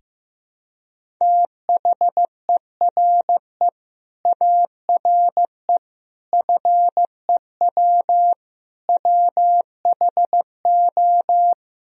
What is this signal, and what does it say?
Morse code, heard as a keyed tone, about 15 wpm, 700 Hz. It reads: THERE ARE FEW WHO
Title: Sentences from Top 1000 Words - Review Part 1 - 15wpm